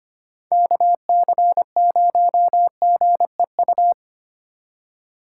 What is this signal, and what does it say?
Morse code, heard as a keyed tone, about 31 wpm, 700 Hz. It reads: KC0GEU
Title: Speed Racing - USA Call Signs 25wpm